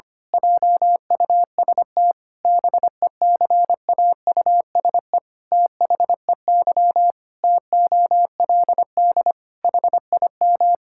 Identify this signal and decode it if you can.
Morse code, heard as a keyed tone, about 25 wpm, 700 Hz. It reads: JUST BECAUSE THEY TOLD HIM
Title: Pangram Challenge v2 - 25wpm